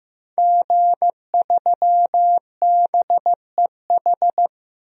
Morse code, heard as a keyed tone, about 15 wpm, 700 Hz. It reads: G3BEH
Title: DXCC-100 Call Signs - 5 Characters - 15wpm